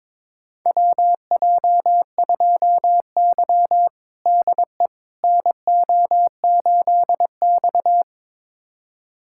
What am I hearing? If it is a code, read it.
Morse code, heard as a keyed tone, about 21 wpm, 700 Hz. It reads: WJ2Y DE NO8X
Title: USA Callsign DE Callsign - 4 Characters - 22wpm